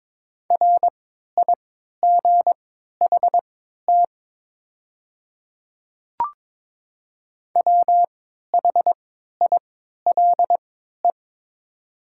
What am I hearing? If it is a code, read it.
Morse code, heard as a keyed tone, about 22 wpm, spaced as for 13 wpm, 700 Hz. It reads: RIGHT WHILE
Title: 3x Character Spacing - Top 300 Words - 22wpm